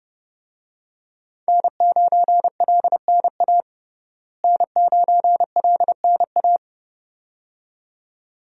Morse code, heard as a keyed tone, about 30 wpm, 700 Hz. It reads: N9LNA N9LNA
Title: USA Call Signs - 5 Characters - Repeated 2x - 30wpm